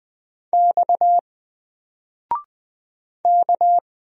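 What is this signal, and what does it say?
Morse code, heard as a keyed tone, about 20 wpm, 700 Hz. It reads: X K
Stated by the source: Single letter and number Morse drill at 20 wpm, from N4LCD.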